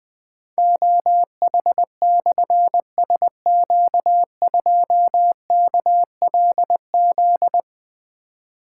Morse code, heard as a keyed tone, about 20 wpm, 700 Hz. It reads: OH/SQ2KLZ
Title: Difficult Call Signs 20wpm